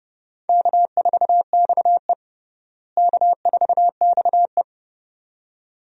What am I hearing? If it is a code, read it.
Morse code, heard as a keyed tone, about 30 wpm, 700 Hz. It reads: K4XE K4XE
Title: USA Call Signs - 4 Characters - Repeated 2x - 30wpm